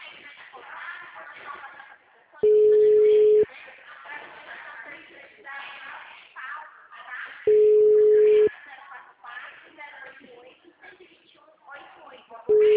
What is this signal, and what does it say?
Ringback tone (Europe, 425 Hz), with speech.